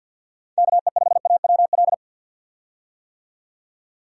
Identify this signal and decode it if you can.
Morse code, heard as a keyed tone, about 47 wpm, 700 Hz. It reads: KE5AWL
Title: USA Call Signs - 6 Characters - 50wpm